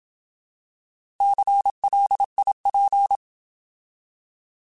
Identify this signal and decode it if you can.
Morse code, heard as a keyed tone, about 27 wpm, 800 Hz. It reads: CLIP
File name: Alphabetical / Alphabetical mp3s / Cs @25